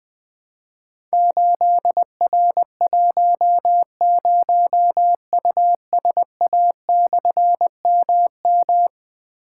Morse code, heard as a keyed tone, about 21 wpm, 700 Hz. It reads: 8R10USA/MM
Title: Difficult Call Signs 20wpm